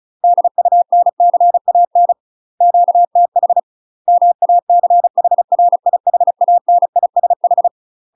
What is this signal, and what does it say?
Morse code, heard as a keyed tone, about 36 wpm, 700 Hz. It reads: DUNCAN QTH MACHRIHANISH
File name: UK MASTER QSOs TRAINING / 35WPM / UK MASTER QSOs 0018